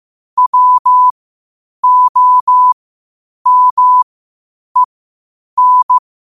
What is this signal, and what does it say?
Morse code, heard as a keyed tone, about 15 wpm, spaced as for 8 wpm, 1000 Hz. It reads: WOMEN
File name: qotdfiles / 8wpm